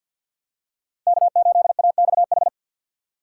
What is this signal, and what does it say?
Morse code, heard as a keyed tone, about 52 wpm, 700 Hz. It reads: K8AXH
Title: USA Call Signs - 5 Characters - 3x Character Spacing - 50wpm